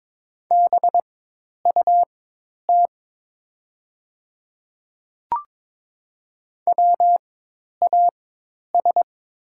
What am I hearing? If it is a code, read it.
Morse code, heard as a keyed tone, about 22 wpm, spaced as for 10 wpm, 700 Hz. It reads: BUT WAS
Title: 4x Character Spacing - Top 100 Words - 22wpm